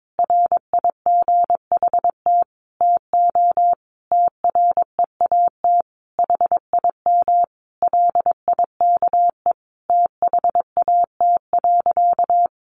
Morse code, heard as a keyed tone, about 22 wpm, 700 Hz. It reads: RIGHT TO TREAT HIM LIKE THAT.
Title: Sentences from Top 1300 Words 22wpm